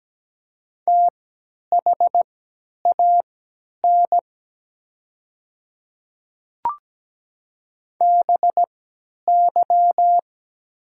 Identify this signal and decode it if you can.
Morse code, heard as a keyed tone, about 17 wpm, spaced as for 10 wpm, 700 Hz. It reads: THAN BY
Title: 3x Character Spacing - Top 200 Words - 17wpm